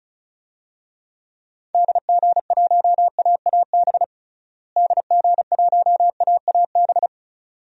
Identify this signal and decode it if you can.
Morse code, heard as a keyed tone, about 36 wpm, 700 Hz. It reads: DG1AAB DG1AAB
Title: DXCC-100 Call Signs - 6 Characters - Repeated 2x - 35wpm